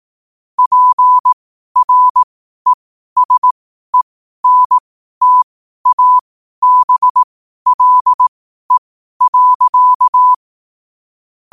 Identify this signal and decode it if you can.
Morse code, heard as a keyed tone, about 18 wpm, spaced as for 12 wpm, 1000 Hz. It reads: PRESENTABLE.
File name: qotdfiles / 12wpm